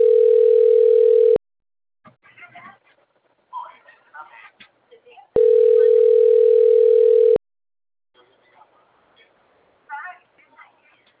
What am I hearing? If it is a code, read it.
Ringback tone (North America), 440 + 480 Hz, with speech.
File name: out- - - - - .87991